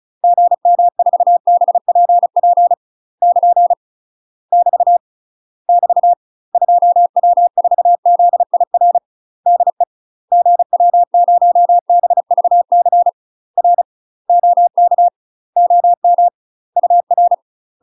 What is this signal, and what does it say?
Morse code, heard as a keyed tone, about 36 wpm, 700 Hz. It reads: GM4BPP <KN> <BT> <BT> 2W4ZIR DE GW0BVC R OK OM UR
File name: UK MASTER QSOs TRAINING / 35WPM / UK MASTER QSOs 0009